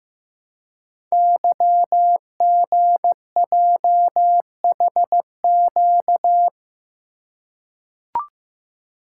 Morse code, heard as a keyed tone, about 15 wpm, 700 Hz. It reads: YGJHQ
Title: Code Groups - 5 Letters - 15wpm